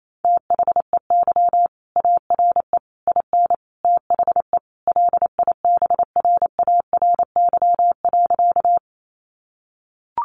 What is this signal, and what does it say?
Morse code, heard as a keyed tone, about 28 wpm, 700 Hz. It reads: THEY ARE IN THE LIBRARY.
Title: Sentences from Top 600 Words 28wpm